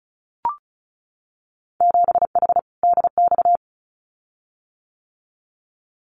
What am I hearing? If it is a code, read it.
Morse code, heard as a keyed tone, about 36 wpm, 700 Hz. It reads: 7H DX
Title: Winter Field Day Exchanges Encore 35wpm